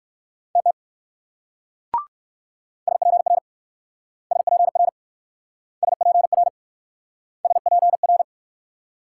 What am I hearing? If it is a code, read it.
Morse code, heard as a keyed tone, about 62 wpm, 700 Hz. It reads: TT SPR SPR SPR SPR